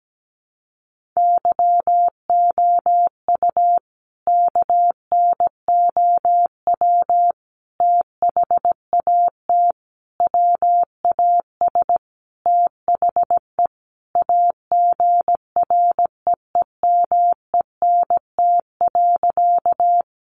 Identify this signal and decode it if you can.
Morse code, heard as a keyed tone, about 17 wpm, 700 Hz. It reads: YOU KNOW THAT WAS THE AGREEMENT.
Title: Sentences from Top 1500 Words 17wpm